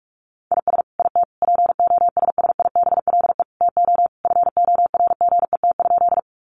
Morse code, heard as a keyed tone, about 44 wpm, 700 Hz. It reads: IS IT POSSIBLE TO FORGET?
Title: Sentences from Top 1200 Words 45wpm